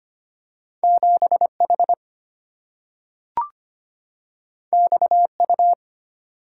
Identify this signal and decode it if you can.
Morse code, heard as a keyed tone, about 25 wpm, 700 Hz. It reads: 7 H X U
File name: Two Letter-Number Combinations - 25wpm